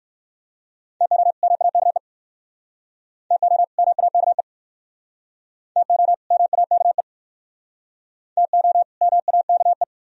Morse code, heard as a keyed tone, about 67 wpm, 700 Hz. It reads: TO MAKE TO MAKE TO MAKE TO MAKE